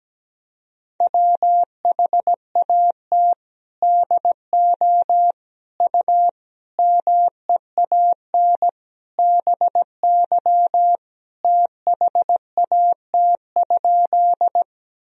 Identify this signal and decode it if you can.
Morse code, heard as a keyed tone, about 17 wpm, 700 Hz. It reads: WHAT DO U MEAN BY THAT?
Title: Sentences from Top 200 Words and Abbreviations 17wpm